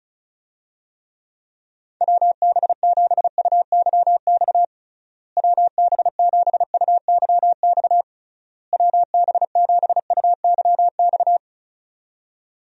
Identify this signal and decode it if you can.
Morse code, heard as a keyed tone, about 36 wpm, 700 Hz. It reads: WB7UYX WB7UYX WB7UYX
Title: USA Call Signs - 6 Characters - Repeated 3x - 35wpm